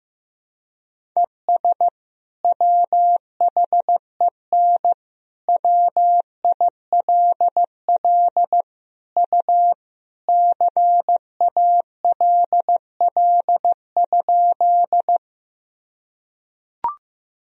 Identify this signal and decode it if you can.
Morse code, heard as a keyed tone, about 15 wpm, 700 Hz. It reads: ES WHEN WILL U CALL?
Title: Sentences from Top 100 Words and Abbreviations 15wpm